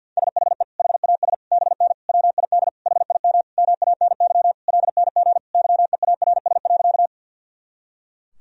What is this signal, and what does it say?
Morse code, heard as a keyed tone, about 50 wpm, 700 Hz. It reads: SHE HAS BN WID HIM MANY LNG YEARS.
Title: Sentences from Top 200 Words and Abbreviations 50wpm